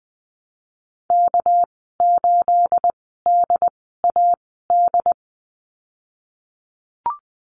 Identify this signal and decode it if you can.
Morse code, heard as a keyed tone, about 20 wpm, spaced as for 15 wpm, 700 Hz. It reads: K8DAD
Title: 213 Lesson 34g - Number 8 - Call Signs 20wpm